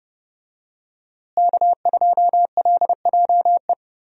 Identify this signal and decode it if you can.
Morse code, heard as a keyed tone, about 29 wpm, 700 Hz. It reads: K2LJE
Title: USA Call Signs - 5 Characters - Repeated 3x - 30wpm